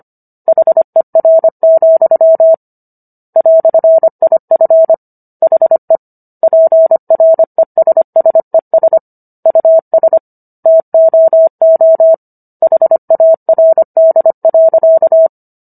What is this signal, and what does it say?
Morse code, heard as a keyed tone, about 25 wpm, 650 Hz. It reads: HER, "IF HE PRESSES US TOO HARD.